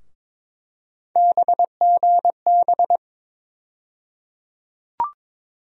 Morse code, heard as a keyed tone, about 22 wpm, 700 Hz. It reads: BGB
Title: USA Call Sign Postfixes - 3 Letters 22wpm